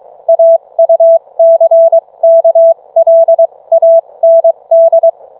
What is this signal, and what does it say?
Morse code, heard as a keyed tone, about 22 wpm, spaced as for 19 wpm, 660 Hz. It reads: AUCKLAND